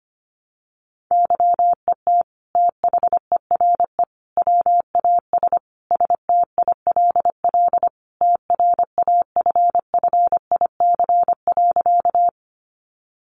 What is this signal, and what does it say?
Morse code, heard as a keyed tone, about 25 wpm, 700 Hz. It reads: YET THERE WAS STILL TRAFFIC.
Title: Sentences from Top 1300 Words 25wpm